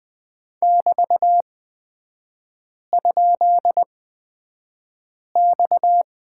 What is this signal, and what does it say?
Morse code, heard as a keyed tone, about 20 wpm, spaced as for 5 wpm, 690 Hz. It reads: <BT>?X
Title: 229 Lesson 37e - Letter x - Three Characters 20wpm